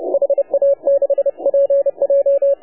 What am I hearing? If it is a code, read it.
Morse code, heard as a keyed tone, about 32 wpm, 500 Hz, with radio noise. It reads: HA6PJ
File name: AUDIO2019 / HA7UI CW